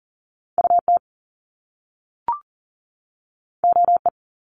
Morse code, heard as a keyed tone, about 40 wpm, 700 Hz. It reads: UT OE
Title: Top Two Letter Combinations in Words 40wpm